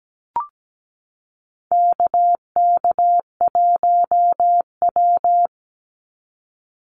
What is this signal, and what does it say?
Morse code, heard as a keyed tone, about 17 wpm, 700 Hz. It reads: KK1W
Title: NA SOTA Chaser Callsigns 17wpm